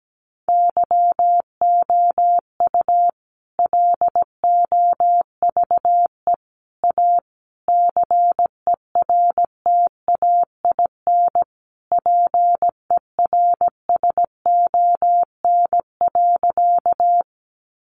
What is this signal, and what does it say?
Morse code, heard as a keyed tone, about 17 wpm, 700 Hz. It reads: YOU LOVE A CERTAIN PERSON.
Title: Sentences from Top 500 Words 17wpm